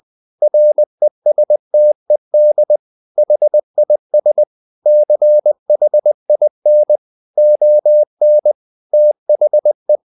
Morse code, heard as a keyed tone, about 20 wpm, 600 Hz. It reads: RESTED HIS CHIN ON THE
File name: the old man and the sea part 2 20wpm